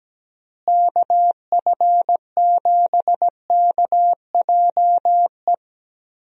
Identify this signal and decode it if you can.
Morse code, heard as a keyed tone, about 17 wpm, 700 Hz. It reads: KF7KJE
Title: USA Call Signs - 6 Characters - Repeated 2x - 17wpm